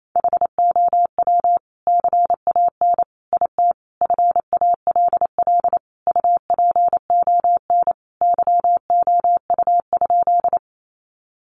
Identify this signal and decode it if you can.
Morse code, heard as a keyed tone, about 28 wpm, 700 Hz. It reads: HOW CAN IT FALL UPON YOU?
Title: Sentences from Top 1100 Words 28wpm